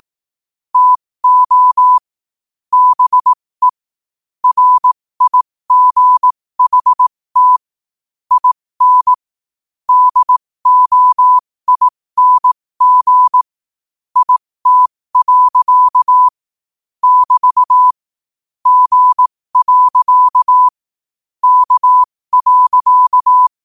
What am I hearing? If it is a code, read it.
Morse code, heard as a keyed tone, about 18 wpm, spaced as for 15 wpm, 1000 Hz. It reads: TO BE RIGHT IN DOING IT. <BT> G. K.
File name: qotdfiles / 15wpm